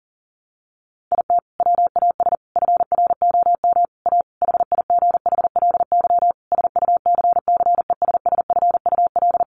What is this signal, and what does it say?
Morse code, heard as a keyed tone, about 40 wpm, 700 Hz. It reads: IT WAS FROM A HIGHLY SUCCESSFUL
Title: Sentences from Top 1100 Words 40wpm